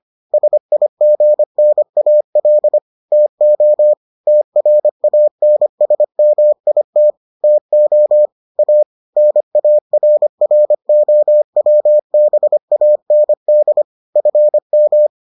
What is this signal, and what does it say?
Morse code, heard as a keyed tone, about 25 wpm, 600 Hz. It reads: SIGNAL TO TRANSMIT TO A NARROWBAND FM